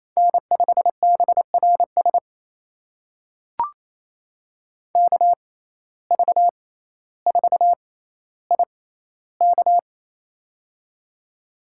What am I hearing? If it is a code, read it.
Morse code, heard as a keyed tone, about 28 wpm, spaced as for 14 wpm, 700 Hz. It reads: N5BRS KV4IK
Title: USA Call Signs - 5 Characters - 6x Character Spacing - 28wpm